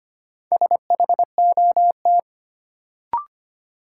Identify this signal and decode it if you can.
Morse code, heard as a keyed tone, about 25 wpm, 700 Hz. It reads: SHOT
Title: Sentences from Top 1100 Words - New Word Drill - 25wpm